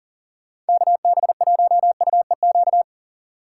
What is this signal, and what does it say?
Morse code, heard as a keyed tone, about 48 wpm, 700 Hz. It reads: KB1UEQ
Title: Speed Racing - USA 2x3 Call Signs 40wpm